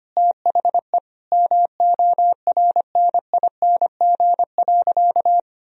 Morse code, heard as a keyed tone, about 25 wpm, 700 Hz. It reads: THE MORNING.
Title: Sentences from Top 1000 Words - Review Part 1 - 25wpm